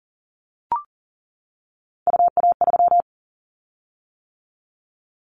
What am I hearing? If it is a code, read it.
Morse code, heard as a keyed tone, about 41 wpm, 700 Hz. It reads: UA3
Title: Callsign Prefixes for Top 100 DXCC - 40wpm